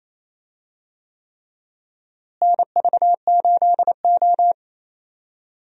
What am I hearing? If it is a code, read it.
Morse code, heard as a keyed tone, about 28 wpm, 700 Hz. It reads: NV8O
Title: USA Call Signs - 4 Characters - Repeated 3x - 28wpm